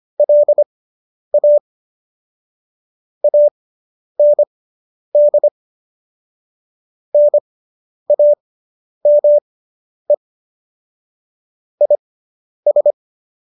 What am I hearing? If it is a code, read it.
Morse code, heard as a keyed tone, about 25 wpm, spaced as for 10 wpm, 590 Hz. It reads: LA AND NAME IS